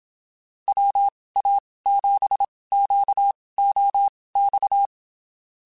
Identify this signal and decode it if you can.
Morse code, heard as a keyed tone, about 27 wpm, spaced as for 19 wpm, 800 Hz. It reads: WA7QOX
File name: QSOs Spaced